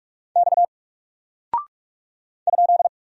Morse code, heard as a keyed tone, about 44 wpm, 700 Hz. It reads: X ?
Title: Single Character (Letter:Number:Punctuation:Prosign) Alt 45wpm